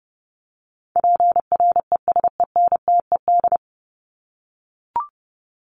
Morse code, heard as a keyed tone, about 30 wpm, 700 Hz. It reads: PRESENTED
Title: ED Suffix - Top 100 Words with Base Words 30wpm